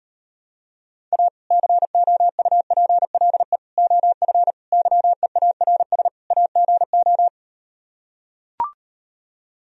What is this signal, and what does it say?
Morse code, heard as a keyed tone, about 38 wpm, 700 Hz. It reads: A COUPLE OF YEARS AGO